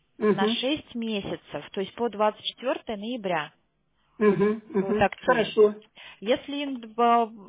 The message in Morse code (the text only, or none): none